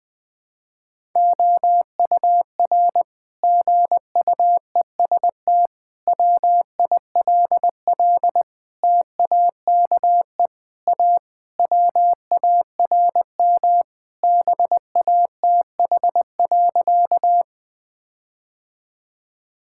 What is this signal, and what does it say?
Morse code, heard as a keyed tone, about 20 wpm, 700 Hz. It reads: OUR GUEST WILL TAKE A WARM BATH.
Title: Sentences from Top 1600 Words 20wpm